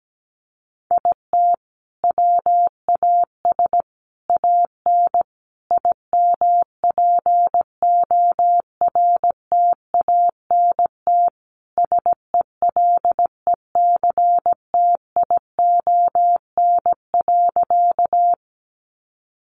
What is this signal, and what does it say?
Morse code, heard as a keyed tone, about 17 wpm, 700 Hz. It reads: IT WAS AN IMPORTANT SELECTION.
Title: Sentences from Top 1400 Words 17wpm